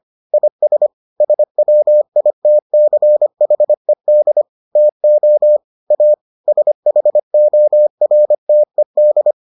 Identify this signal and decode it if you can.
Morse code, heard as a keyed tone, about 25 wpm, 600 Hz. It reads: IS SWITCHED TO A SHORTED